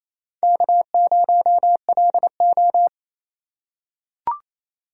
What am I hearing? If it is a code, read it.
Morse code, heard as a keyed tone, about 28 wpm, 700 Hz. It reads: K0LO
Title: USA Call Signs - 4 Characters - 4x Character Spacing - 28wpm